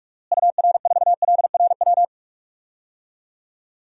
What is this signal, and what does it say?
Morse code, heard as a keyed tone, about 41 wpm, 700 Hz. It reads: UR4LRW
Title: DXCC-100 Call Signs - 6 Characters - Repeated 3x - 45wpm